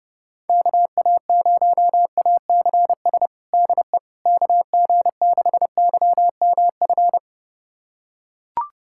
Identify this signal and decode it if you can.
Morse code, heard as a keyed tone, about 32 wpm, 690 Hz. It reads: KA0ACS DE KG6YMF
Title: USA Callsign DE Callsign 30wpm